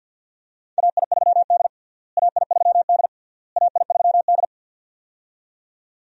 Morse code, heard as a keyed tone, about 51 wpm, 700 Hz. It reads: AI3D AI3D AI3D
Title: USA Call Signs - 4 Characters - Repeated 3x - 50wpm